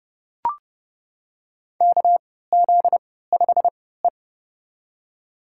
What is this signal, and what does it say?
Morse code, heard as a keyed tone, about 30 wpm, spaced as for 17 wpm, 700 Hz. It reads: KZ5E
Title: USA Call Signs - 3x Character Spacing - 30wpm